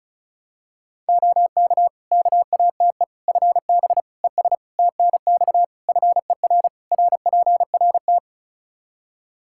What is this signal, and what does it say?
Morse code, heard as a keyed tone, about 36 wpm, 700 Hz. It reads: OK KATE FB ES TNX FER RPRT
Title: Practice QSO - Exchange 2 - Info - 35wpm